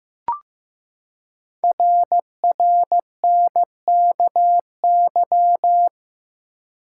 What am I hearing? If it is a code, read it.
Morse code, heard as a keyed tone, about 15 wpm, 700 Hz. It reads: RRNKY
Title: Code Groups - 5 Letters - 15wpm